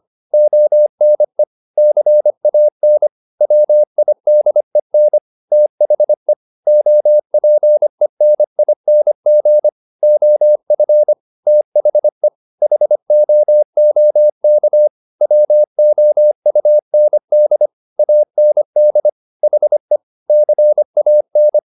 Morse code, heard as a keyed tone, about 25 wpm, 600 Hz. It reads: ONE CAN WIDEN THE OPENING OF THE HOOK WOUND AND HE CAN